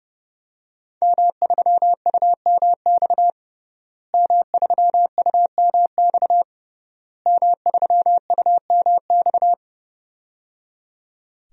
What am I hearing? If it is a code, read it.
Morse code, heard as a keyed tone, about 30 wpm, 700 Hz. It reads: M3UMX M3UMX M3UMX
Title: DXCC-100 Call Signs - 5 Characters - Repeated 3x - 30wpm